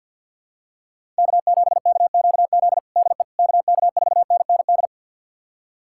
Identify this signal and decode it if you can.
Morse code, heard as a keyed tone, about 53 wpm, 690 Hz. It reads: K7KQZ DE KK4NND
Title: USA Callsign DE Callsign 50wpm